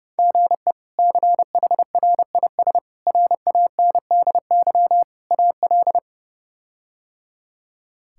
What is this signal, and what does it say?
Morse code, heard as a keyed tone, about 30 wpm, 700 Hz. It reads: GE CHRIS RANDY AL
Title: K1USN SST QSO Practice - 30wpm